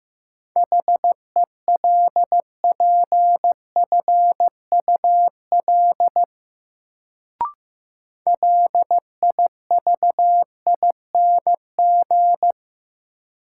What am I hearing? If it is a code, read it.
Morse code, heard as a keyed tone, about 15 wpm, 700 Hz. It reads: HELPFUL LIVING
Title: Sentences from Top 1000 Words - Review Part 6 - New Word Drill - 15wpm